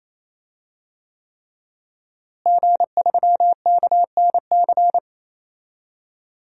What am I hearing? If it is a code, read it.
Morse code, heard as a keyed tone, about 28 wpm, 700 Hz. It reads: G3KNC
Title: DXCC-100 Call Signs - 5 Characters - Repeated 3x - 28wpm